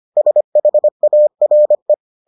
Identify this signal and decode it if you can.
Morse code, heard as a keyed tone, about 25 wpm, 600 Hz. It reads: SHARE